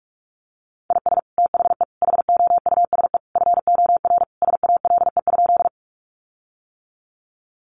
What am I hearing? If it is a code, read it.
Morse code, heard as a keyed tone, about 44 wpm, 700 Hz. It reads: IS THE HOUSE FOR SALE?
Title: Sentences from Top 1400 Words 45wpm